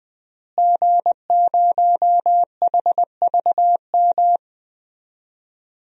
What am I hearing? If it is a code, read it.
Morse code, heard as a keyed tone, about 21 wpm, 700 Hz. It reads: G0HVM
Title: DXCC-100 Call Signs - 5 Characters - Repeated 3x - 20wpm